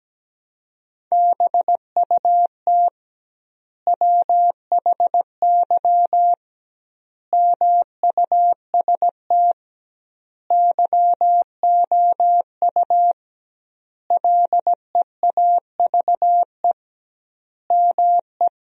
Morse code, heard as a keyed tone, about 17 wpm, 700 Hz. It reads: BUT WHY MUST YOU LEAVE ME
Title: 2x Word-Spacing - Sentences from Top 400 Words - 17wpm